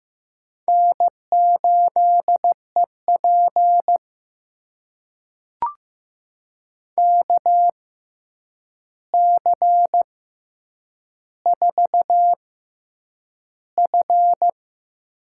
Morse code, heard as a keyed tone, about 15 wpm, spaced as for 8 wpm, 700 Hz. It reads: N8EP KC4F
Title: USA Call Signs - 6x Character Spacing - 15wpm